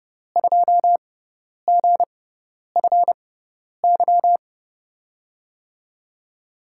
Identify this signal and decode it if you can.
Morse code, heard as a keyed tone, about 30 wpm, spaced as for 10 wpm, 700 Hz. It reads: 2GFY
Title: USA Call Signs - 6 Characters - 6x Character Spacing - 30wpm